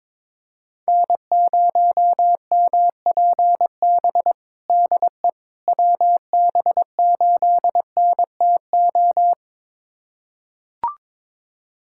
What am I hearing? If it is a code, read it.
Morse code, heard as a keyed tone, about 22 wpm, 700 Hz. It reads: N0MPB DE WB8NTO
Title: USA Callsign DE Callsign 22wpm